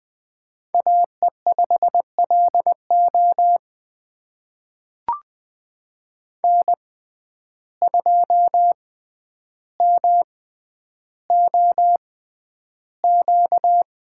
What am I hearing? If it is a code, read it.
Morse code, heard as a keyed tone, about 20 wpm, spaced as for 10 wpm, 700 Hz. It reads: AE5LO N2MOQ
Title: USA Call Signs - 5 Characters - 6x Character Spacing - 20wpm